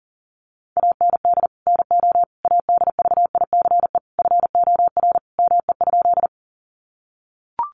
Morse code, heard as a keyed tone, about 40 wpm, 700 Hz. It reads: AND NO ADVICE FOR ME?
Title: Sentences from Top 1100 Words 40wpm